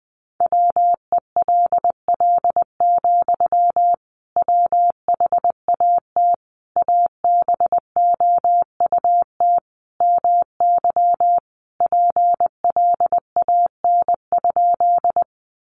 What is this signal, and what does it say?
Morse code, heard as a keyed tone, about 20 wpm, 700 Hz. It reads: WELL, WHAT ABOUT MY PLAN?
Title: Sentences from Top 500 Words 20wpm